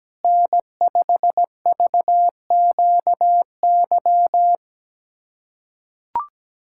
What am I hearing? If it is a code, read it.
Morse code, heard as a keyed tone, about 17 wpm, 700 Hz. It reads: N5VQY
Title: USA Callsign DE Callsign - 5 Characters - 17wpm